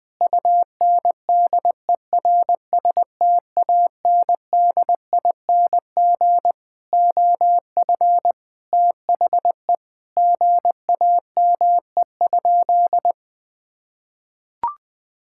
Morse code, heard as a keyed tone, about 20 wpm, 700 Hz. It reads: UNDERSTANDING OF THE GAME?
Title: Sentences from Top 1000 Words - Review Part 5 - 20wpm